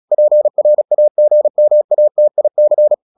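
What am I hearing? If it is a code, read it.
Morse code, heard as a keyed tone, about 36 wpm, 600 Hz. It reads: PRAGMATIC
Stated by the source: Morse practice at 36 wpm, text of Ken Follett's novel Never, from CW med Gnister.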